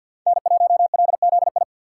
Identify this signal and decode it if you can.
Morse code, heard as a keyed tone, about 53 wpm, 700 Hz. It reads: N1LZI
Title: USA Call Signs - 3x Character Spacing - 50wpm